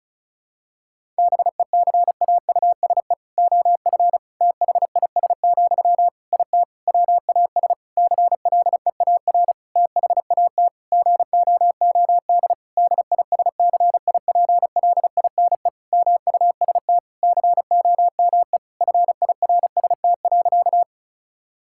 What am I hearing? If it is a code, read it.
Morse code, heard as a keyed tone, about 36 wpm, 700 Hz. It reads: BECAUSE OF THIS, IT WAS CLEAR THAT GOOD DISCIPLINE MUST COME FIRST.
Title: Sentences from Top 1000 Words - Review Part 6 - 35wpm